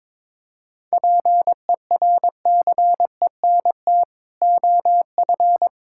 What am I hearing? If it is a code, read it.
Morse code, heard as a keyed tone, about 21 wpm, 700 Hz. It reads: PERCENT OF
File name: Build A Phrase - 2-Word Phrases - 22wpm